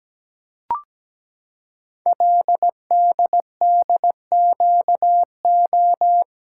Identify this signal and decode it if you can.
Morse code, heard as a keyed tone, about 17 wpm, 700 Hz. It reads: LDDQO
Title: Code Groups - 5 Letters - 17wpm